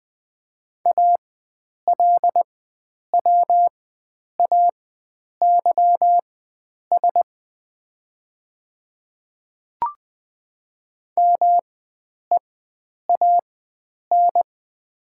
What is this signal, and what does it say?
Morse code, heard as a keyed tone, about 21 wpm, spaced as for 9 wpm, 700 Hz. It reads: ALWAYS MEAN
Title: 4x Character Spacing - Top 300 Words - 20wpm